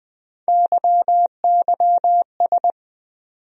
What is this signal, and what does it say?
Morse code, heard as a keyed tone, about 20 wpm, 700 Hz. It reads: YYS
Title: USA Call Sign Postfixes - 3 Letters 20wpm